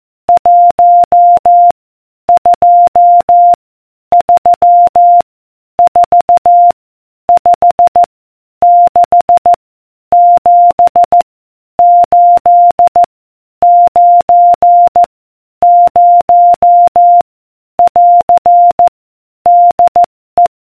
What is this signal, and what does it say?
Morse code, heard as a keyed tone, about 14 wpm, 700 Hz. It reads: 1 2 3 4 5 6 7 8 9 0 <AR> DE